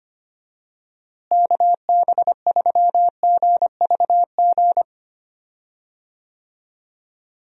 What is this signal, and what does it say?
Morse code, heard as a keyed tone, about 25 wpm, 700 Hz. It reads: KB3GVG
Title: USA Call Signs - 6 Characters - Repeated 3x - 25wpm